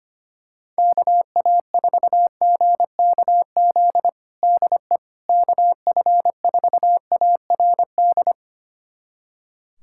Morse code, heard as a keyed tone, about 25 wpm, 700 Hz. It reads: KA4GKZ DE KF4ARD
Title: USA Callsign DE Callsign - 6 Characters - 25wpm